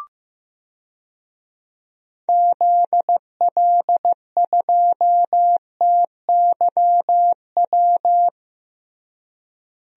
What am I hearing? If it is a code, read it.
Morse code, heard as a keyed tone, about 15 wpm, 700 Hz. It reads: ZL2TYW
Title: DXCC-100 Call Signs - 6 Characters - Repeated 2x - 15wpm